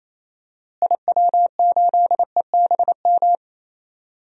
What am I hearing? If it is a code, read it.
Morse code, heard as a keyed tone, about 28 wpm, 700 Hz. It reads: IW8EBM
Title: DXCC-100 Call Signs - 6 Characters - Repeated 3x - 28wpm